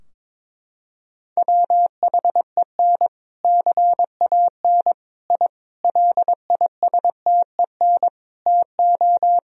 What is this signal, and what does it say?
Morse code, heard as a keyed tone, about 21 wpm, 700 Hz. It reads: WHEN CAN I LISTEN TO
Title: Sentences from Top 1000 Words - Review Part 3 - 22wpm